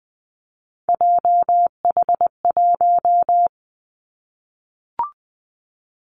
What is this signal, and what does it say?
Morse code, heard as a keyed tone, about 20 wpm, 700 Hz. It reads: JH1
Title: Callsign Prefixes for Top 25 DXCC - 20wpm